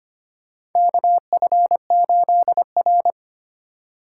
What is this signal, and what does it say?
Morse code, heard as a keyed tone, about 25 wpm, 700 Hz. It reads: KF8R
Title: USA Call Signs - 3x Character Spacing - 25wpm